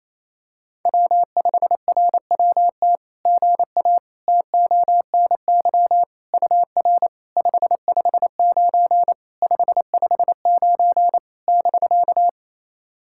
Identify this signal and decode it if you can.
Morse code, heard as a keyed tone, about 29 wpm, 660 Hz. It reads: W5RWT GA TONY UR 559 559 <BK>
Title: SOTA QSO Practice 28wpm